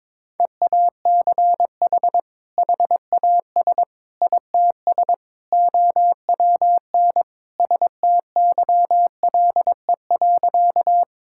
Morse code, heard as a keyed tone, about 22 wpm, 700 Hz. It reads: EACH HAS ITS OWN STYLE.
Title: Sentences from Top 1000 Words - Review Part 5 - 22wpm